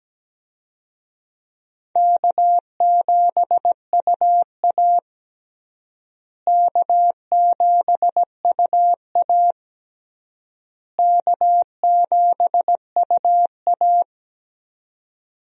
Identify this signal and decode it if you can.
Morse code, heard as a keyed tone, about 17 wpm, 700 Hz. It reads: K7UA K7UA K7UA
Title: USA Call Signs - 4 Characters - Repeated 3x - 17wpm